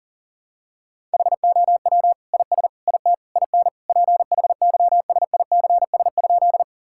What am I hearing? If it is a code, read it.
Morse code, heard as a keyed tone, about 40 wpm, 700 Hz. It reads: HOW IS IT IN PHYSICS?
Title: Sentences from Top 1000 Words - Review Part 5 - 40wpm